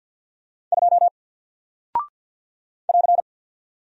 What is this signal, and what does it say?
Morse code, heard as a keyed tone, about 48 wpm, 700 Hz. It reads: 2 <AR>
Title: Single Character (Letter:Number:Punctuation:Prosign) Alt 50wpm